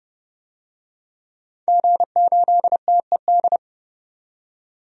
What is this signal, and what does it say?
Morse code, heard as a keyed tone, about 32 wpm, 700 Hz. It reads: G8TED
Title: DXCC-100 Call Signs - 5 Characters - 30wpm